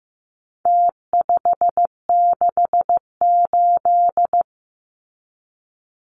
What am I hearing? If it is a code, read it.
Morse code, heard as a keyed tone, about 15 wpm, 700 Hz. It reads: T568
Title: Numbers - Contest Style - 4-Digit 15wpm